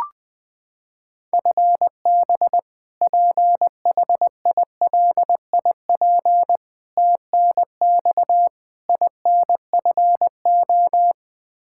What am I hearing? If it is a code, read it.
Morse code, heard as a keyed tone, about 20 wpm, 700 Hz. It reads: FB PHILIP TNX INFO
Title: Practice QSO - Exchange 3 - More Info - 20wpm